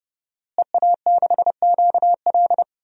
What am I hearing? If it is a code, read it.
Morse code, heard as a keyed tone, about 32 wpm, 700 Hz. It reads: EA6QL
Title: DXCC-100 Call Signs - 5 Characters - Repeated 2x - 30wpm